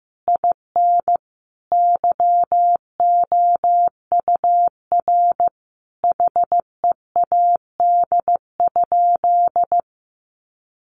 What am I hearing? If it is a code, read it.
Morse code, heard as a keyed tone, about 15 wpm, 700 Hz. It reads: IN YOUR HEAD?
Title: Sentences from Top 1500 Words 15wpm